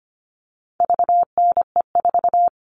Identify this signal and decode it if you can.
Morse code, heard as a keyed tone, about 25 wpm, 700 Hz. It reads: VNE4
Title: All Cut Numbers - 4-Digit 25wpm